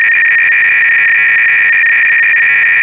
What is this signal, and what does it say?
300-baud modem data.